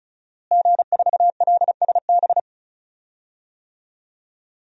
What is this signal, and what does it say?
Morse code, heard as a keyed tone, about 36 wpm, 700 Hz. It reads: G4LSB
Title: DXCC-100 Call Signs - 5 Characters - Repeated 3x - 35wpm